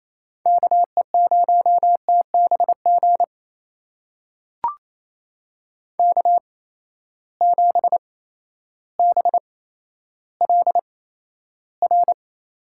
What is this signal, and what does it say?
Morse code, heard as a keyed tone, about 28 wpm, spaced as for 13 wpm, 700 Hz. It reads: KE0TBG K7BLR
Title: USA Call Signs - 8x Character Spacing - 28wpm